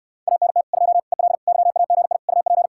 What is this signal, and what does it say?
Morse code, heard as a keyed tone, about 69 wpm, 700 Hz. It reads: ANT 2 EL YAGI UP